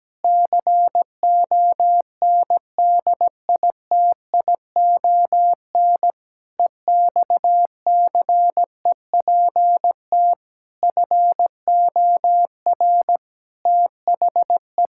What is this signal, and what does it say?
Morse code, heard as a keyed tone, about 17 wpm, 700 Hz. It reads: CONDITION EXCEPT FOR THE
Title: Pangram Challenge v2 - 17wpm